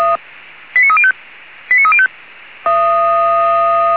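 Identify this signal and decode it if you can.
Five-tone selective call (ZVEI), heard as quick runs of five tones, tones of about 70 ms. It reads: 89305 89305
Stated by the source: ZVEI five-tone selective call 89297 received via SDR.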